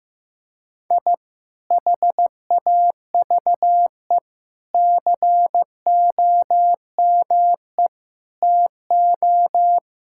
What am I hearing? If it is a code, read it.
Morse code, heard as a keyed tone, about 15 wpm, 700 Hz. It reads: I HAVE COME TO
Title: Sentences from Top 1000 Words - Review Part 6 - 15wpm